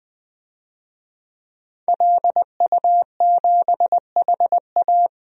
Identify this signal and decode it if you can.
Morse code, heard as a keyed tone, about 20 wpm, 700 Hz. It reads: LU7HA
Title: DXCC-100 Call Signs - 5 Characters - 20wpm